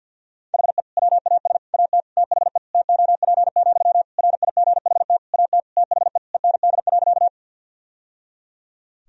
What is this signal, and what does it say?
Morse code, heard as a keyed tone, about 51 wpm, 700 Hz. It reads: HE WAS AT THE TOP, RIGHT AT THE END.
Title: Sentences from Top 1000 Words - Review Part 2 - 50wpm